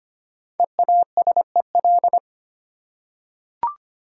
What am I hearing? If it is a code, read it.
Morse code, heard as a keyed tone, about 25 wpm, 690 Hz. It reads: EASEL
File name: practice / 25wpm / Easy Words with Hints - 25wpm